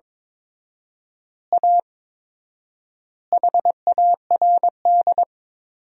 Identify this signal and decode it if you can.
Morse code, heard as a keyed tone, about 22 wpm, 700 Hz. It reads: A HARD
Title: 4x Word-Spacing - Sentences from Top 300 Words - 22wpm